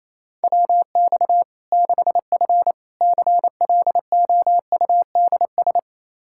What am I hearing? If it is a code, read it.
Morse code, heard as a keyed tone, about 27 wpm, 700 Hz. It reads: WX 6F CLOUDS
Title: Practice QSO - Exchange 2 - Info - 28wpm